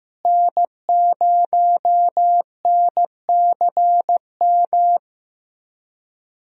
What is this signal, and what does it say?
Morse code, heard as a keyed tone, about 15 wpm, 700 Hz. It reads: N0NCM
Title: USA Call Signs - 5 Characters - Repeated 3x - 15wpm